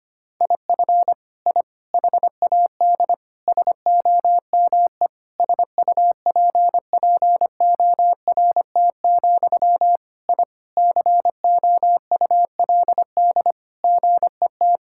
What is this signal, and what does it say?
Morse code, heard as a keyed tone, about 25 wpm, 700 Hz. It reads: IF I HAD SOME SUPPORT, I COULD GET
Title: Sentences from Top 1000 Words - Review Part 5 - 25wpm